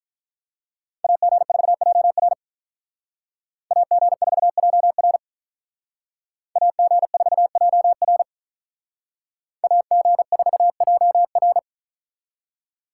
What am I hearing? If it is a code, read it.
Morse code, heard as a keyed tone, about 43 wpm, 700 Hz. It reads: AG4JR AG4JR AG4JR AG4JR
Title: Speed Racing - USA 2x2 Call Signs 35wpm